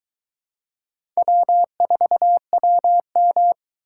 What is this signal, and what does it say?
Morse code, heard as a keyed tone, about 22 wpm, 700 Hz. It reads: W4WM